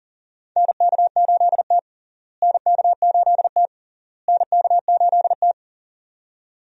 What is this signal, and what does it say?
Morse code, heard as a keyed tone, about 40 wpm, 700 Hz. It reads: NK8T NK8T NK8T
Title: USA Call Signs - 4 Characters - Repeated 3x - 40wpm